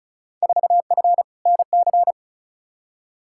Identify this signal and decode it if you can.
Morse code, heard as a keyed tone, about 36 wpm, 700 Hz. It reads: 4F NC
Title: Field Day Exchanges - 35wpm